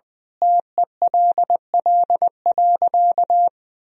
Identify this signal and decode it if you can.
Morse code, heard as a keyed tone, about 20 wpm, 700 Hz. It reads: TELL.
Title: Sentences from Top 1000 Words - Review Part 4 - 20wpm